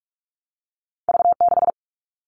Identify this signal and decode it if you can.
Morse code, heard as a keyed tone, about 46 wpm, 700 Hz. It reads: V6
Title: All Cut Numbers - 2-Digit 45wpm